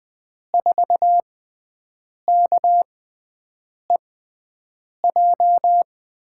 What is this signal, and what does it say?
Morse code, heard as a keyed tone, about 20 wpm, spaced as for 7 wpm, 700 Hz. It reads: 4KEJ
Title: USA Call Signs - 5 Characters - 6x Character Spacing - 20wpm